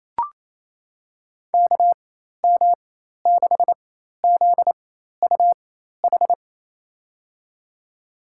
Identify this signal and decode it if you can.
Morse code, heard as a keyed tone, about 28 wpm, spaced as for 13 wpm, 700 Hz. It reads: KM6ZUH
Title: USA Call Signs - 6 Characters - 4x Character Spacing - 28wpm